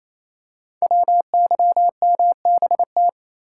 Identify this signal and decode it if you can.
Morse code, heard as a keyed tone, about 27 wpm, 700 Hz. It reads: WYMBT
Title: Code Groups - 5 Letters - 28wpm